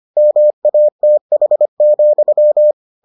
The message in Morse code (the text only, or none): MATH,